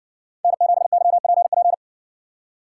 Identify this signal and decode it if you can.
Morse code, heard as a keyed tone, about 60 wpm, 700 Hz. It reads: N7YPP